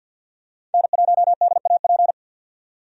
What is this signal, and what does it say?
Morse code, heard as a keyed tone, about 53 wpm, 700 Hz. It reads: N1DAP
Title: USA Call Signs - 5 Characters - 50wpm